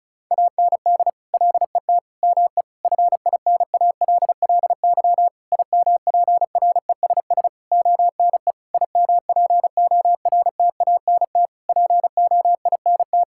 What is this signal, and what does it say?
Morse code, heard as a keyed tone, about 34 wpm, 700 Hz. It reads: AND LET ME FINALLY IMPRESS ONE IMPORTANT POINT
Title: Sentences from Top 1600 Words 35wpm